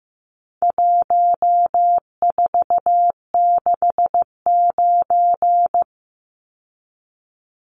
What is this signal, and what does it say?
Morse code, heard as a keyed tone, about 15 wpm, 700 Hz. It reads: 1469
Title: Numbers - 4-Digit 15wpm